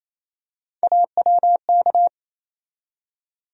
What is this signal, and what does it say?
Morse code, heard as a keyed tone, about 28 wpm, 700 Hz. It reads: AWK
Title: USA Call Sign Postfixes - 3 Letters 28wpm